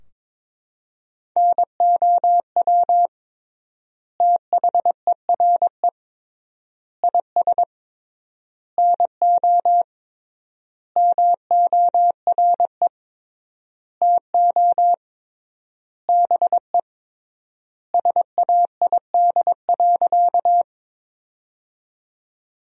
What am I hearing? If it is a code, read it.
Morse code, heard as a keyed tone, about 22 wpm, 700 Hz. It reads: NOW THERE IS NO MORE TO BE SAID.
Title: 3x Word-Spacing - Sentences from Top 100 Words - 22wpm